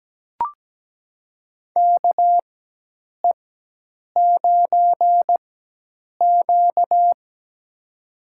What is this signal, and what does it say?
Morse code, heard as a keyed tone, about 17 wpm, spaced as for 8 wpm, 700 Hz. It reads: KE9Q
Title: USA Call Signs - 4 Characters - 4x Character Spacing - 17wpm